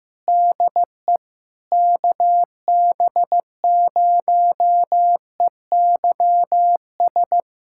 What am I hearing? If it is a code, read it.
Morse code, heard as a keyed tone, about 15 wpm, 700 Hz. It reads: DE KB0EYS
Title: USA Callsign DE Callsign - 6 Characters - 15wpm